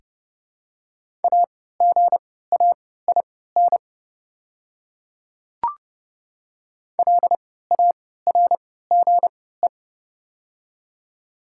Morse code, heard as a keyed tone, about 32 wpm, spaced as for 17 wpm, 700 Hz. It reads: AGAIN LARGE
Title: 3x Character Spacing - Top 200 Words - 30wpm